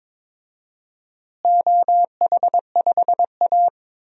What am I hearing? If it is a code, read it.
Morse code, heard as a keyed tone, about 22 wpm, 700 Hz. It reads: OH5A
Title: DXCC-100 Call Signs - 4 Characters - Repeated 3x - 22wpm